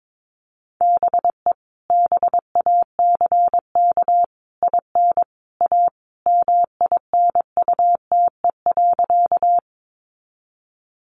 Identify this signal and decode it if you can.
Morse code, heard as a keyed tone, about 22 wpm, 700 Hz. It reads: BE BACK IN A MINUTE.
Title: Sentences from Top 500 Words 22wpm